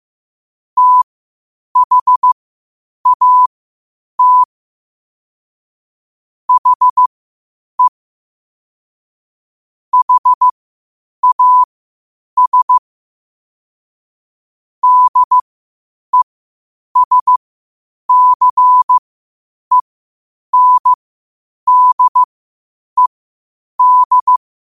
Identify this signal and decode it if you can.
Morse code, heard as a keyed tone, about 15 wpm, spaced as for 8 wpm, 1000 Hz. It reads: THAT HE HAS DESCENDED
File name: qotdfiles / 8wpm